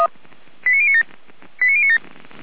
Five-tone selective call (ZVEI), heard as quick runs of five tones, tones of about 70 ms. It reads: 89097 89097
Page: ZVEI